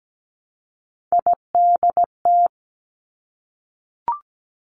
Morse code, heard as a keyed tone, about 17 wpm, 700 Hz. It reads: IDT
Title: Three Challenging Letter-Number Combinations 17wpm